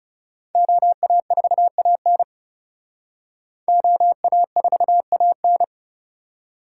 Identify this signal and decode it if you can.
Morse code, heard as a keyed tone, about 35 wpm, 700 Hz. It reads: OA4AN OA4AN
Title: Speed Racing - USA 2x2 Call Signs 30wpm